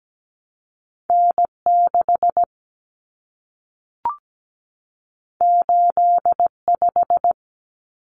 Morse code, heard as a keyed tone, about 17 wpm, 700 Hz. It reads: N6 85
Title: Common Cut Numbers - 2-Digit 17wpm